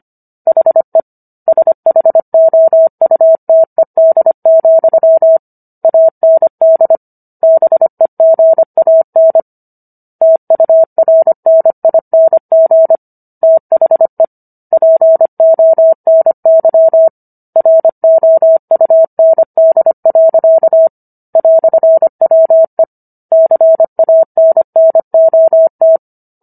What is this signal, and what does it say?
Morse code, heard as a keyed tone, about 25 wpm, 650 Hz. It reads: HE SHOUTED, AND BEGAN TURNING THE PONY ROUND. "WE CANNOT